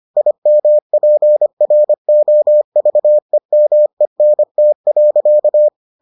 Morse code, heard as a keyed tone, about 25 wpm, 600 Hz. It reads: IMPROVEMENT.